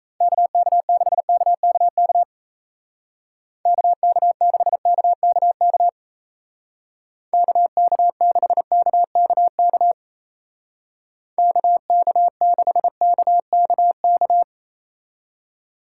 Morse code, heard as a keyed tone, about 35 wpm, 700 Hz. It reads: KK6KKK KK6KKK KK6KKK KK6KKK